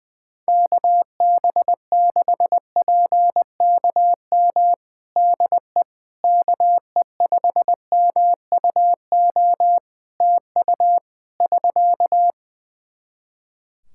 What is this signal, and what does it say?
Morse code, heard as a keyed tone, about 21 wpm, 700 Hz. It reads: KB6PKM DE KE5MUO TU <SK>
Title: Practice QSO - Exchange 4 - Ending - 20wpm